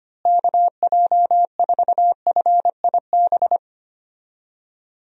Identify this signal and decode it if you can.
Morse code, heard as a keyed tone, about 25 wpm, 700 Hz. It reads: KJ4FIB
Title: USA Call Signs - 6 Characters - Repeated 3x - 25wpm